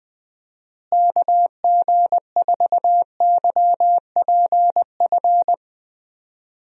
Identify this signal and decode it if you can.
Morse code, heard as a keyed tone, about 20 wpm, 700 Hz. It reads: KG4YPF
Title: USA Call Signs - 6 Characters - Repeated 3x - 20wpm